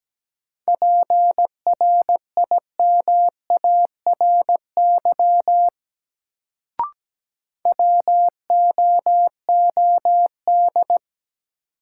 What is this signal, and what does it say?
Morse code, heard as a keyed tone, about 17 wpm, 700 Hz. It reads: PRIMARY WOOD
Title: Sentences from Top 1100 Words - New Word Drill Alt - 17wpm